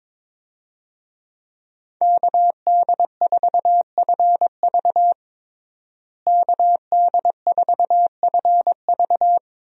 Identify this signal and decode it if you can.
Morse code, heard as a keyed tone, about 22 wpm, 700 Hz. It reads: KD4FV KD4FV
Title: USA Call Signs - 5 Characters - Repeated 2x - 22wpm